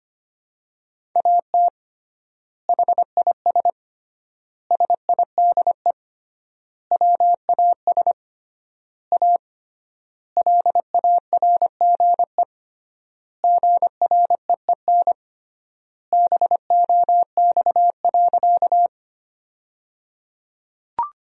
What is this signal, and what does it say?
Morse code, heard as a keyed tone, about 25 wpm, 700 Hz. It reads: AT HIS SIDE WAS A LARGE GREEN BOX.
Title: 3x Word-Spacing - Sentences from Top 500 Words - 25wpm